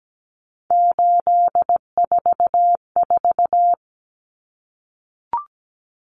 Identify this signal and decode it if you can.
Morse code, heard as a keyed tone, about 17 wpm, 700 Hz. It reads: 844
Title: Common Cut Numbers - 3-Digit 17wpm